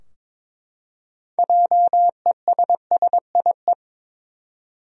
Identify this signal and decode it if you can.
Morse code, heard as a keyed tone, about 21 wpm, 700 Hz. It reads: JESSIE
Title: USA Common Names - 22wpm